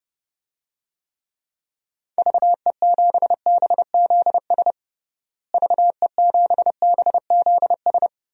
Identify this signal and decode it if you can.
Morse code, heard as a keyed tone, about 30 wpm, 700 Hz. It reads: VE7BZS VE7BZS
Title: DXCC-100 Call Signs - 6 Characters - Repeated 2x - 30wpm